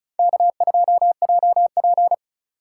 Morse code, heard as a keyed tone, about 36 wpm, 700 Hz. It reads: K2JP
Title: Speed Racing - USA 1x2 Call Signs 30wpm